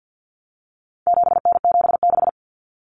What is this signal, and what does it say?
Morse code, heard as a keyed tone, about 50 wpm, 700 Hz. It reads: 7N76
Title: Common Cut Numbers - 4-Digit 50wpm